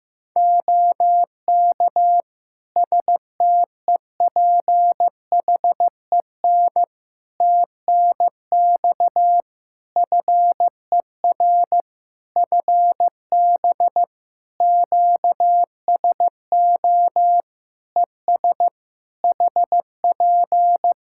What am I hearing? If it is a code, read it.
Morse code, heard as a keyed tone, about 15 wpm, 700 Hz. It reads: OK STEPHEN TNX FER FB QSO ES HP
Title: K1USN SST QSO Practice 15wpm